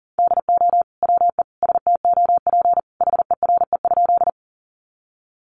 Morse code, heard as a keyed tone, about 40 wpm, 700 Hz. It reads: DO WE STOP HERE?
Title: Sentences from Top 300 Words 40wpm